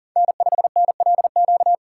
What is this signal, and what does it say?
Morse code, heard as a keyed tone, about 42 wpm, 700 Hz. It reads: N5NLQ
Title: USA Call Signs - 5 Characters - 6x Character Spacing - 40wpm